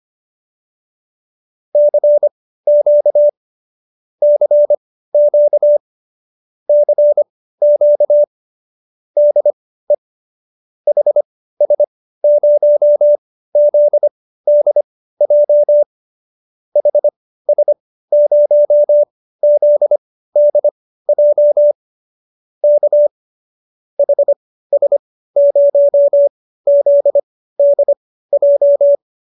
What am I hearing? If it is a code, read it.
Morse code, heard as a keyed tone, about 25 wpm, spaced as for 15 wpm, 590 Hz. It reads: CQ CQ CQ DE HS0ZDJ HS0ZDJ K HS0ZDJ